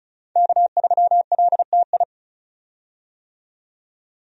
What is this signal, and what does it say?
Morse code, heard as a keyed tone, about 36 wpm, 700 Hz. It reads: K3LTI
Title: USA Call Signs - 5 Characters - 35wpm